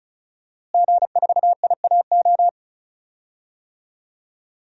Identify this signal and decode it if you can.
Morse code, heard as a keyed tone, about 36 wpm, 700 Hz. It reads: G4IAO
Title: DXCC-100 Call Signs - 5 Characters - Repeated 2x - 35wpm